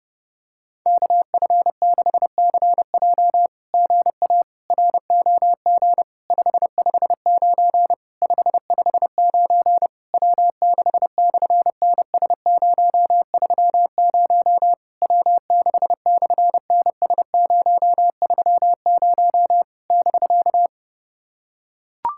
Morse code, heard as a keyed tone, about 31 wpm, 700 Hz. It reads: KF6CJ GA ROG 559 559 W6/NS030 W6/NS030 <BK>
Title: SOTA QSO Practice 30wpm